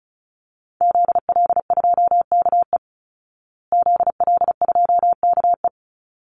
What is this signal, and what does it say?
Morse code, heard as a keyed tone, about 34 wpm, 700 Hz. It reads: ZL2KE ZL2KE
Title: Call Signs Repeated 3x - 35wpm